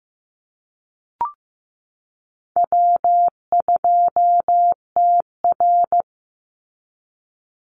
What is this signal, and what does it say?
Morse code, heard as a keyed tone, about 15 wpm, 700 Hz. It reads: W2TR
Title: Easy Call Signs Repeated 3x - 15wpm